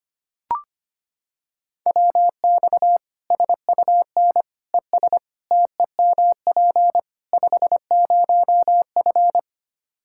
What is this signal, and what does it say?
Morse code, heard as a keyed tone, about 25 wpm, 700 Hz. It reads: WX SUN ES TEMP 50F
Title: Practice QSO - Exchange 2 - Info - 25wpm